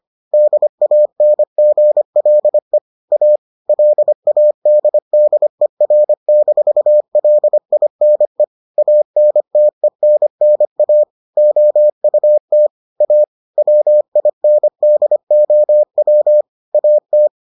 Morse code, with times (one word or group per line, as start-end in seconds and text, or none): 0.3-2.8 DANGLE
3.1-3.4 A
3.7-8.4 LADDER-LINE
8.8-11.0 ANTENNA
11.4-12.7 OUT
13.0-13.2 A
13.6-16.4 WINDOW
16.7-17.3 AT